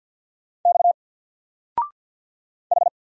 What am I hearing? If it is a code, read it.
Morse code, heard as a keyed tone, about 50 wpm, 700 Hz. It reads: X H
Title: Single Letter-Number 50wpm